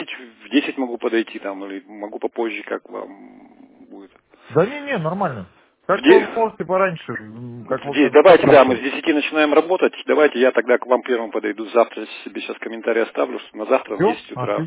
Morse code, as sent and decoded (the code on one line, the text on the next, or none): none